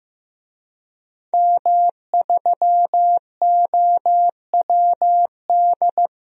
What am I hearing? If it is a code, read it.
Morse code, heard as a keyed tone, about 15 wpm, 700 Hz. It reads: M3OWD
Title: DXCC-100 Call Signs - 5 Characters - 15wpm